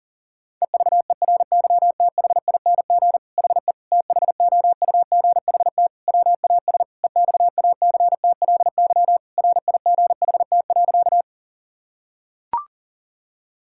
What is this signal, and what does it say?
Morse code, heard as a keyed tone, about 40 wpm, 700 Hz. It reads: EVERYTHING HE THOUGHT WAS EXACTLY RIGHT.
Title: Sentences from Top 1000 Words - Review Part 6 - 40wpm